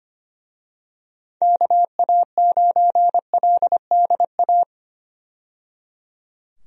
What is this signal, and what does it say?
Morse code, heard as a keyed tone, about 25 wpm, 700 Hz. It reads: KA9LDA
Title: USA Call Signs - 6 Characters - 25wpm